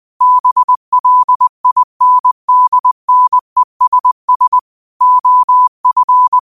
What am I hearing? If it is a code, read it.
Morse code, heard as a keyed tone, about 20 wpm, 1000 Hz. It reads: BLINDNESS OF